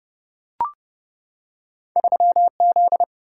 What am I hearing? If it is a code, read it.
Morse code, heard as a keyed tone, about 30 wpm, 700 Hz. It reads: 3Z
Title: Single-Letter Single-Number 30wpm